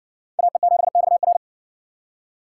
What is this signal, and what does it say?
Morse code, heard as a keyed tone, about 60 wpm, 700 Hz. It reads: AE7XR